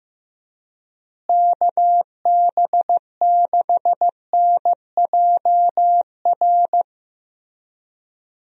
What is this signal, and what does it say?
Morse code, heard as a keyed tone, about 15 wpm, 700 Hz. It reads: KB6NJR
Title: USA Call Signs - 6 Characters - Repeated 3x - 15wpm